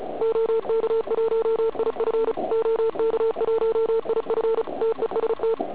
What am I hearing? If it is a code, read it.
Morse code, heard as a keyed tone, about 33 wpm, 440 Hz, with noise. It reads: OK1IF OK1IF TEST
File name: Archiv / CUC106 / cuc106 PILEUP